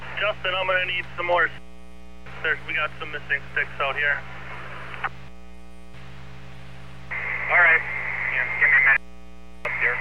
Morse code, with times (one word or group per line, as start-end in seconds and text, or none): none